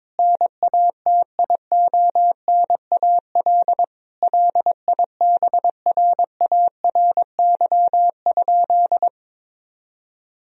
Morse code, heard as a keyed tone, about 22 wpm, 700 Hz. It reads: NATIONAL LIBRARY?
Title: Sentences from Top 1000 Words - Review Part 4 - 22wpm